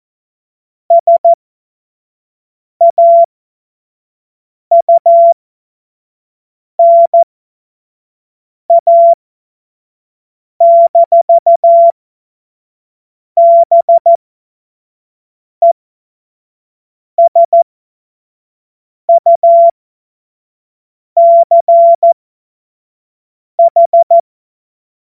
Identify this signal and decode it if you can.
Morse code, heard as a keyed tone, about 14 wpm, spaced as for 5 wpm, 680 Hz. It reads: SAUNA-BESUCH
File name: Morsekurs / Witze / Witz 13. Juli 2021, 25 F70